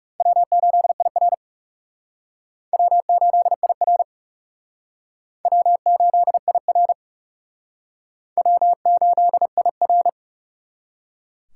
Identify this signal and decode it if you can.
Morse code, heard as a keyed tone, about 36 wpm, 700 Hz. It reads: W8IR W8IR W8IR W8IR
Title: Speed Racing - USA 1x2 Call Signs 30wpm